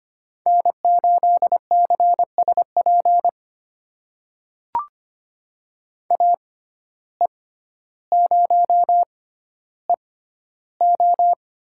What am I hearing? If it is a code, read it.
Morse code, heard as a keyed tone, about 25 wpm, spaced as for 13 wpm, 700 Hz. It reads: N8CSP AE0EO
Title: USA Call Signs - 5 Characters - 6x Character Spacing - 25wpm